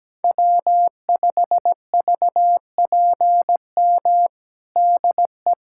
Morse code, heard as a keyed tone, about 17 wpm, 700 Hz. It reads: W5VPM DE
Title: USA Callsign DE Callsign - 5 Characters - 17wpm